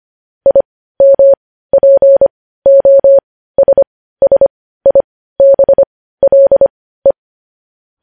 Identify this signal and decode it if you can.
Morse code, heard as a keyed tone, about 25 wpm, spaced as for 15 wpm, 550 Hz. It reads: IMPOSSIBLE
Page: news headlines in morse code at 15 wpm